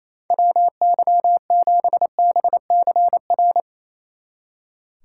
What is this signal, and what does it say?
Morse code, heard as a keyed tone, about 34 wpm, 700 Hz. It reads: WY7BCR
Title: Speed Racing - USA 2x3 Call Signs 28wpm